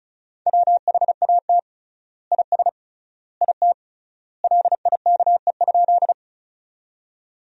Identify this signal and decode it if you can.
Morse code, heard as a keyed tone, about 36 wpm, 700 Hz. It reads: WHAT IS IT LIKE?
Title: 3x Word-Spacing - Sentences from Top 100 Words - 35wpm